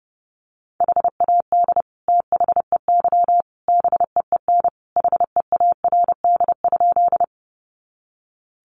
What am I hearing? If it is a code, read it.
Morse code, heard as a keyed tone, about 30 wpm, 700 Hz. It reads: HAD THEY BEEN HEARD?
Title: Sentences from Top 400 Words 30wpm